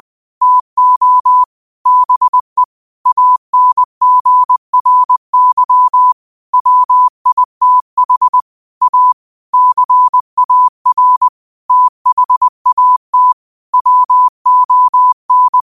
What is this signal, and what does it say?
Morse code, heard as a keyed tone, about 20 wpm, 1000 Hz. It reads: TO BE ANGRY WITH A CAR THAT WON
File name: qotdfiles / 20wpm